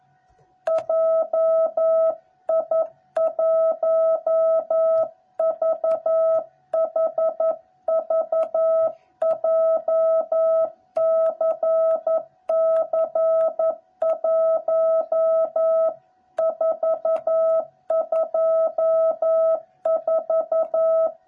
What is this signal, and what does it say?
Morse code, heard as a keyed tone, about 11 wpm, 660 Hz. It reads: JI1VHV JCC1424
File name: jh8cbh / 2021cw / cw06b